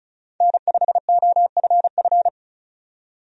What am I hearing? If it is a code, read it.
Morse code, heard as a keyed tone, about 34 wpm, 700 Hz. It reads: N5OFF
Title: USA Call Signs - 5 Characters - 35wpm